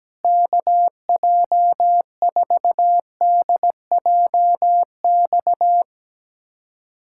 Morse code, heard as a keyed tone, about 17 wpm, 700 Hz. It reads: KJ4DJX
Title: USA Callsign DE Callsign - 6 Characters - 17wpm